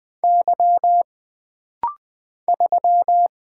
Morse code, heard as a keyed tone, about 20 wpm, 700 Hz. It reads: Y 3
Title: Single Letter-Number Alt - Rapid-Fire 20wpm